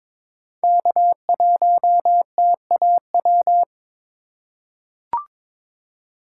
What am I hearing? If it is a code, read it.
Morse code, heard as a keyed tone, about 22 wpm, 700 Hz. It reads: K1TAW
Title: USA Call Signs - 5 Characters - 4x Character Spacing - 22wpm